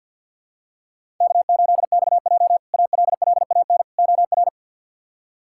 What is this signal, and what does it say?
Morse code, heard as a keyed tone, about 48 wpm, 700 Hz. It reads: K8XJ ALLAN OR
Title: K1USN SST QSO Practice - 50wpm